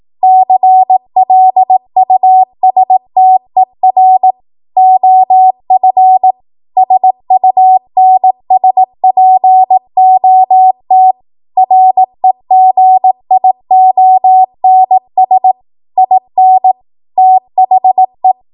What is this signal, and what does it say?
Morse code, heard as a keyed tone, about 18 wpm, 750 Hz. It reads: CLUSTER OF SUNSPOT REGIONS IN THE
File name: Morse / 18WPM